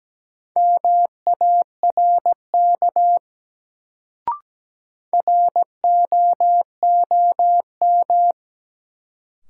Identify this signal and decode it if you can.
Morse code, heard as a keyed tone, about 17 wpm, 700 Hz. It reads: MARK ROOM
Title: Sentences from Top 300 Words - New Word Drill - 17wpm